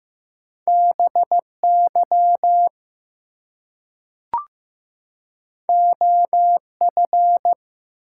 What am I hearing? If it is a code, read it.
Morse code, heard as a keyed tone, about 15 wpm, 700 Hz. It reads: BY OF
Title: Two Letter Words - 15wpm